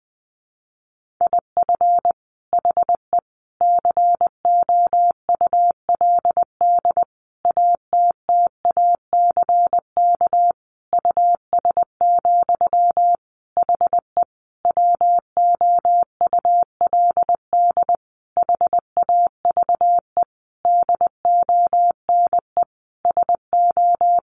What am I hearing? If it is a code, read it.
Morse code, heard as a keyed tone, about 20 wpm, 700 Hz. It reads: IF HE COULD ATTACK US, HE WOULD HAVE DONE SO
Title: Sentences from Top 1500 Words 20wpm